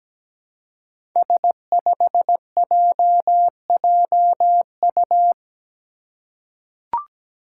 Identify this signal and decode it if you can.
Morse code, heard as a keyed tone, about 17 wpm, 700 Hz. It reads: S5JJU
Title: Five Challenging Letter-Number Combinations 17wpm